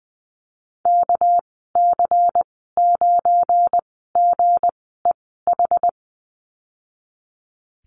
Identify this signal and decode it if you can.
Morse code, heard as a keyed tone, about 20 wpm, spaced as for 14 wpm, 700 Hz. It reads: KC9GEH
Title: 213 Lesson 34g - Number 8 - Call Signs 20wpm